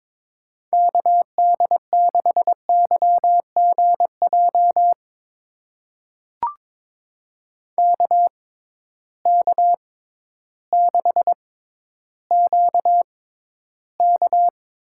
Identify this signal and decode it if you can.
Morse code, heard as a keyed tone, about 22 wpm, spaced as for 12 wpm, 700 Hz. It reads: KD6YGJ KK6QK
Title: USA Call Signs - 6 Characters - 6x Character Spacing - 22wpm